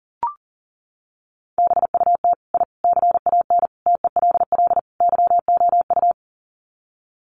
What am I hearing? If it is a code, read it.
Morse code, heard as a keyed tone, about 40 wpm, 700 Hz. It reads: BUT I CAN TELL YOU
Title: Sets of 5 Words Encore 40wpm